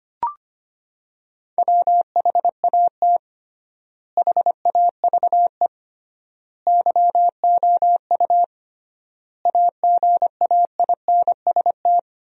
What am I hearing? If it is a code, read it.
Morse code, heard as a keyed tone, about 25 wpm, 700 Hz. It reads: WHAT HAVE YOU AGAINST
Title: 3x Word-Spacing - Sentences from Top 400 Words - 25wpm